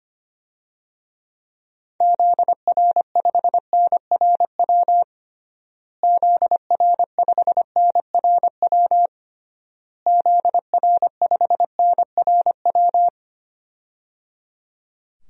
Morse code, heard as a keyed tone, about 25 wpm, 700 Hz. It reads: ZR5NRW ZR5NRW ZR5NRW
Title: DXCC-100 Call Signs - 6 Characters - Repeated 3x - 25wpm